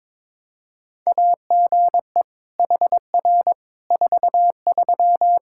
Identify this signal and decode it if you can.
Morse code, heard as a keyed tone, about 21 wpm, 700 Hz. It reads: AGE HR 43
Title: K1USN SST QSO Practice 22wpm